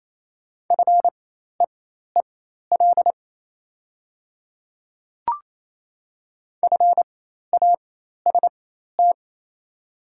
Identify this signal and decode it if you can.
Morse code, heard as a keyed tone, about 28 wpm, spaced as for 13 wpm, 700 Hz. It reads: FEEL FAST
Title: 4x Character Spacing - Top 500 Words - 28wpm